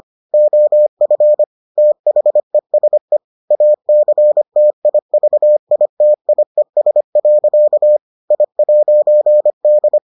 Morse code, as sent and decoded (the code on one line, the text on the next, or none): --- ..-. / - .... ... / .- -.-. - .. ...- .. - .. ... .-.-.- / .. .----. -..
OF THESE ACTIVITIES. I'D